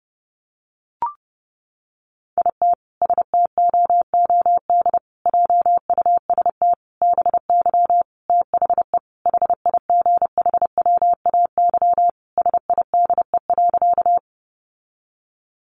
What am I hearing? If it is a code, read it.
Morse code, heard as a keyed tone, about 28 wpm, 700 Hz. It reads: IT STOOD JUST BY THE HIGHWAY SIDE.
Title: Sentences from Top 1400 Words 30wpm